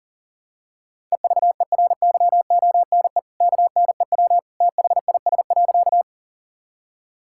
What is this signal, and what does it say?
Morse code, heard as a keyed tone, about 40 wpm, 700 Hz. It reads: EVERYONE KNEW THIS.
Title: Sentences from Top 1000 Words - Review Part 5 - 40wpm